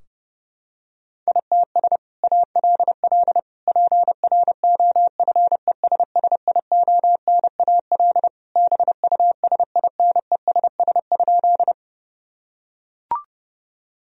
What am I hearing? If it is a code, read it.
Morse code, heard as a keyed tone, about 30 wpm, 700 Hz. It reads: ITS ALL PROFESSIONAL BUSINESS?
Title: Sentences from Top 1000 Words - Review Part 4 - 30wpm